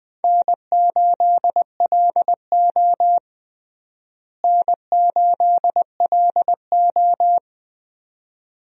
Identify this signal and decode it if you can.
Morse code, heard as a keyed tone, about 20 wpm, 700 Hz. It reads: N8LO N8LO
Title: USA Call Signs - 4 Characters - Repeated 2x - 20wpm